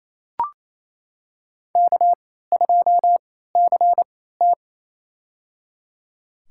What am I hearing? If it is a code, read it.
Morse code, heard as a keyed tone, about 28 wpm, spaced as for 16 wpm, 700 Hz. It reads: K2CT
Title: USA Call Signs - 4 Characters - 3x Character Spacing - 28wpm